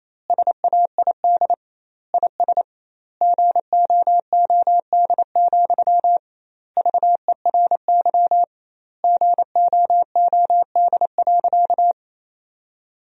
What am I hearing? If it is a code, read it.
Morse code, heard as a keyed tone, about 28 wpm, 700 Hz. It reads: SAID IS GOOD, VERY GOOD.
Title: 2x Word-Spacing - Sentences from Top 200 Words - 28wpm